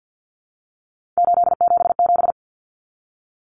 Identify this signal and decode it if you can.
Morse code, heard as a keyed tone, about 50 wpm, 700 Hz. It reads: 877
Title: Common Cut Numbers - 3-Digit 50wpm